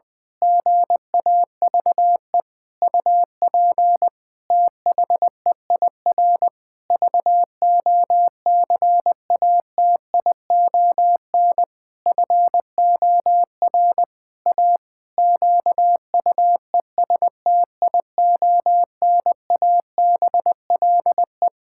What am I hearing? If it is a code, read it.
Morse code, heard as a keyed tone, about 20 wpm, 700 Hz. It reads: GAVE UP THEIR VOCATION FOR A QUESTIONABLE
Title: Pangram Challenge v2 - 20wpm